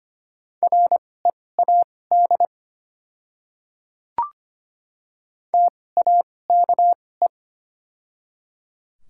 Morse code, heard as a keyed tone, about 25 wpm, spaced as for 18 wpm, 700 Hz. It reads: READ TAKE
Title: 2x Character Spacing - Top 200 Words - 25wpm